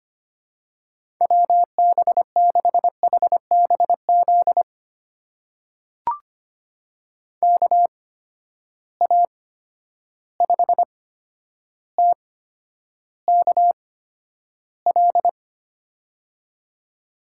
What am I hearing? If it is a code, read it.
Morse code, heard as a keyed tone, about 25 wpm, spaced as for 11 wpm, 700 Hz. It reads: WB6HBZ KA5TKL
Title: USA Call Signs - 6 Characters - 8x Character Spacing - 25wpm